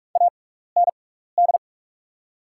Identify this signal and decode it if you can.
Morse code, heard as a keyed tone, about 42 wpm, spaced as for 15 wpm, 700 Hz. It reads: AND
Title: 6x Character Spacing - Top 300 Words - 45wpm